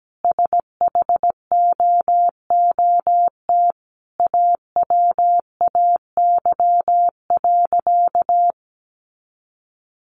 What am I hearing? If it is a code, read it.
Morse code, heard as a keyed tone, about 17 wpm, 700 Hz. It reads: SHOOT AWAY.
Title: Sentences from Top 1200 Words 17wpm